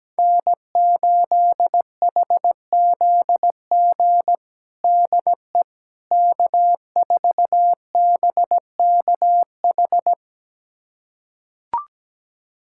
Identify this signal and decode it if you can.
Morse code, heard as a keyed tone, about 17 wpm, 700 Hz. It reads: N8HZG DE K4BKH
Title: USA Callsign DE Callsign 17wpm